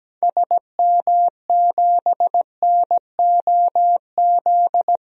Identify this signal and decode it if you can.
Morse code, heard as a keyed tone, about 17 wpm, 700 Hz. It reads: SM7NOZ
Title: DXCC-100 Call Signs - 6 Characters - Repeated 3x - 17wpm